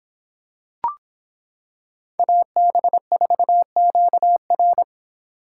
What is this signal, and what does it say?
Morse code, heard as a keyed tone, about 26 wpm, 700 Hz. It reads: AB4QR